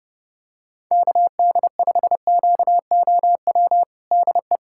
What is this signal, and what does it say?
Morse code, heard as a keyed tone, about 30 wpm, 700 Hz. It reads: KD5QOW DE
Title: USA Callsign DE Callsign 30wpm